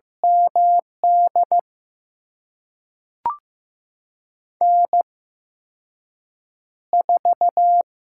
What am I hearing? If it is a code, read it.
Morse code, heard as a keyed tone, about 15 wpm, spaced as for 6 wpm, 700 Hz. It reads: MD N4
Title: USA Call Signs - 8x Character Spacing - 15wpm